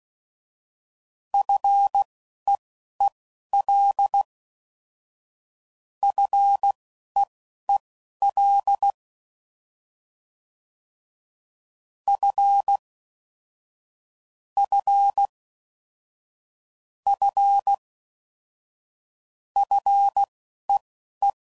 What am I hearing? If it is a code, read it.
Morse code, heard as a keyed tone, about 16 wpm, spaced as for 11 wpm, 800 Hz. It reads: FEEL FEEL F F F FEE